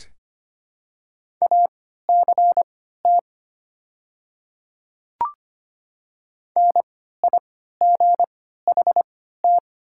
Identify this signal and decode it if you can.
Morse code, heard as a keyed tone, about 25 wpm, spaced as for 14 wpm, 690 Hz. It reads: ACT NIGHT
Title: 3x Character Spacing - Top 300 Words - 25wpm